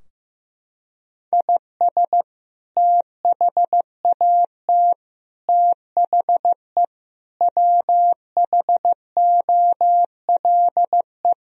Morse code, heard as a keyed tone, about 15 wpm, 700 Hz. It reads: IS THAT THE WHOLE
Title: Sentences from Top 1000 Words - Review Part 4 - 15wpm